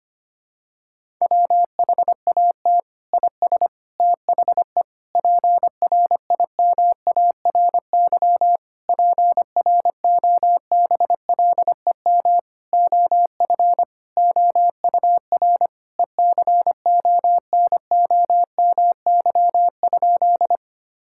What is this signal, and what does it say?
Morse code, heard as a keyed tone, about 25 wpm, 700 Hz. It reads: WHAT IS THE PRIMARY PROBLEM OF OUR ECONOMY?
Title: Sentences from Top 1000 Words - Review Part 5 - 25wpm